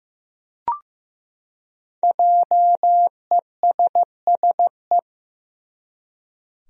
Morse code, heard as a keyed tone, about 15 wpm, 700 Hz. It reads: JESSE
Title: USA Common Names - 15wpm